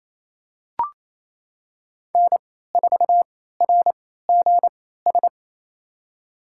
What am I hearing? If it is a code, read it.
Morse code, heard as a keyed tone, about 28 wpm, spaced as for 16 wpm, 700 Hz. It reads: N4RGS
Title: USA Call Signs - 3x Character Spacing - 28wpm